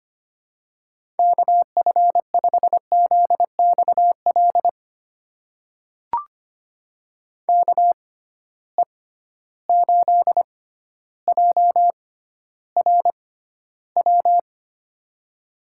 Morse code, heard as a keyed tone, about 25 wpm, spaced as for 13 wpm, 700 Hz. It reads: KF5ZXL KE8JRW
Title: USA Call Signs - 6 Characters - 6x Character Spacing - 25wpm